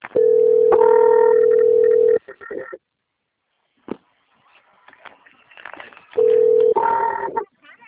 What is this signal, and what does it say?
Ringback tone (North America), 440 + 480 Hz.
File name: out- - - - - .57424